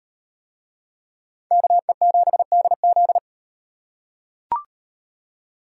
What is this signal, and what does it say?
Morse code, heard as a keyed tone, about 38 wpm, 700 Hz. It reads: KE7DZ